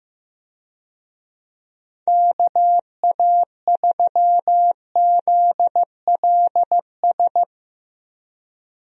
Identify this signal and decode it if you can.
Morse code, heard as a keyed tone, about 15 wpm, 700 Hz. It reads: KA3ZLS
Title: USA Call Signs - 6 Characters - Repeated 3x - 15wpm